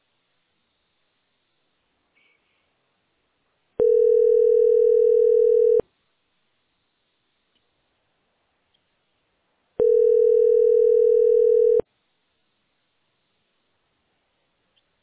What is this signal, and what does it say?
Ringback tone (North America), 440 + 480 Hz.